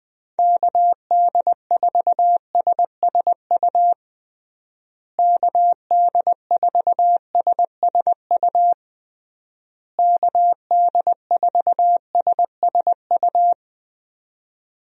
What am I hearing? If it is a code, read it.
Morse code, heard as a keyed tone, about 21 wpm, 700 Hz. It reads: KD4SSU KD4SSU KD4SSU
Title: USA Call Signs - 6 Characters - Repeated 3x - 20wpm